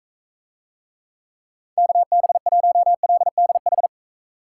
Morse code, heard as a keyed tone, about 45 wpm, 700 Hz. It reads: KB1LDH